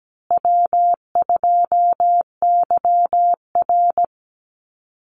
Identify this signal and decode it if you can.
Morse code, heard as a keyed tone, about 17 wpm, 700 Hz. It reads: W2YR
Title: Call Signs Repeated 2x - 17wpm